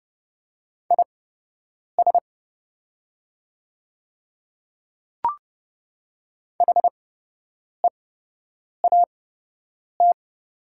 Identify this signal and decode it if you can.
Morse code, heard as a keyed tone, about 30 wpm, spaced as for 8 wpm, 690 Hz. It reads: IS HEAT
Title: 8x Character Spacing - Top 500 Words - 30wpm